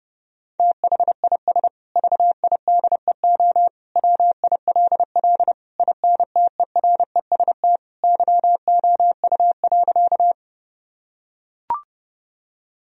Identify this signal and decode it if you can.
Morse code, heard as a keyed tone, about 30 wpm, 700 Hz. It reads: THIS VIDEO WILL INTEREST YOU.
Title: Sentences from Top 1000 Words - Review Part 4 - 30wpm